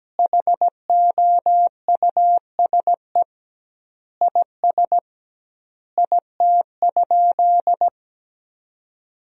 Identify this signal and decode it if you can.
Morse code, heard as a keyed tone, about 17 wpm, 700 Hz. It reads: HOUSE IS IT?
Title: 2x Word-Spacing - Sentences from Top 200 Words - 17wpm